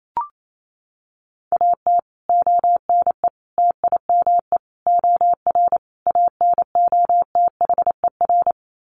Morse code, heard as a keyed tone, about 28 wpm, 700 Hz. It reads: AT ONE TIME OR ANOTHER
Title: Sets of 5 Words Encore 28wpm